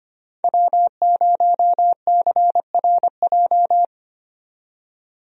Morse code, heard as a keyed tone, about 25 wpm, 700 Hz. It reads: W0CRJ
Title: USA Call Signs - 5 Characters - 25wpm